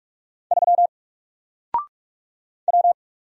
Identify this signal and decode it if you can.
Morse code, heard as a keyed tone, about 44 wpm, 700 Hz. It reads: 3 W
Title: Single Letter-Number Alt 45wpm